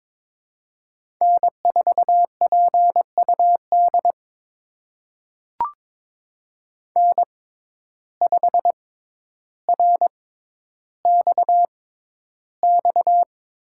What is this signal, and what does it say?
Morse code, heard as a keyed tone, about 23 wpm, spaced as for 11 wpm, 700 Hz. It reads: N4PUD N5RXX
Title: USA Call Signs - 5 Characters - 6x Character Spacing - 22wpm